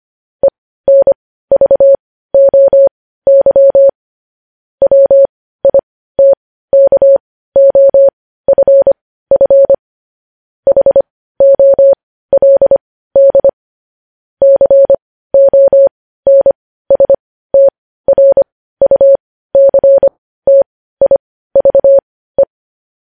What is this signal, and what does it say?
Morse code, heard as a keyed tone, about 25 wpm, spaced as for 15 wpm, 550 Hz. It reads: ENVOY WITKOFF HOLD CONSTRUCTIVE
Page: news-headlines-in-morse-code-at-15-wpm